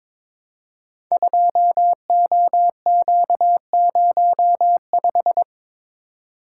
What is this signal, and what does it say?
Morse code, heard as a keyed tone, about 22 wpm, 700 Hz. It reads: 2OQ05
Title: Code Groups - 5 Letter:Numbers - 22wpm